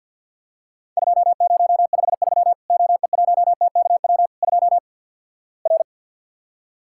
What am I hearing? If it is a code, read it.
Morse code, heard as a keyed tone, about 48 wpm, 700 Hz. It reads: 2053 OE1TKW 2 R
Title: Worked All Europe DX Contest - QTC Exchange - 50wpm